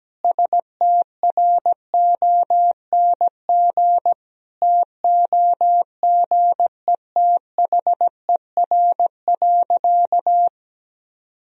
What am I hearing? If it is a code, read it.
Morse code, heard as a keyed tone, about 17 wpm, 700 Hz. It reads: STRONG TOGETHER.
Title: Sentences from Top 500 Words and Abbreviations 17wpm